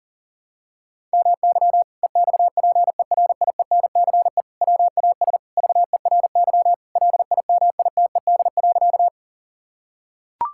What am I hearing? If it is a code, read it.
Morse code, heard as a keyed tone, about 40 wpm, 700 Hz. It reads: MY EXPERIENCE WAS VERY LIMITED.
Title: Sentences from Top 1000 Words - Review Part 6 - 40wpm